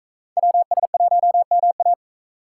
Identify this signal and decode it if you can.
Morse code, heard as a keyed tone, about 44 wpm, 700 Hz. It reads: WS1MA